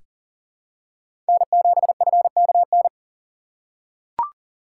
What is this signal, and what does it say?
Morse code, heard as a keyed tone, about 40 wpm, 700 Hz. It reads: N7FKN
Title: USA Call Signs - 5 Characters - 2x Character Spacing - 40wpm